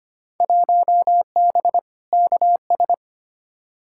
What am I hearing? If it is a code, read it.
Morse code, heard as a keyed tone, about 25 wpm, 700 Hz. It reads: 1B KS
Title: Field Day Exchanges Encore - 25wpm